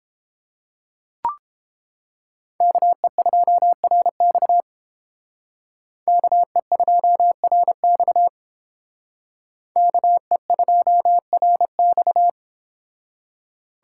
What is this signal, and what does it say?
Morse code, heard as a keyed tone, about 29 wpm, 700 Hz. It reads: KE2RX KE2RX KE2RX